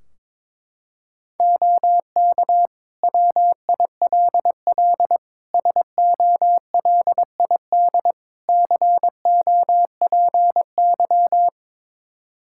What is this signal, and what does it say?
Morse code, heard as a keyed tone, about 23 wpm, 700 Hz. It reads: OK WILL SOLID COPY
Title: Practice QSO - Exchange 3 - More Info - 22wpm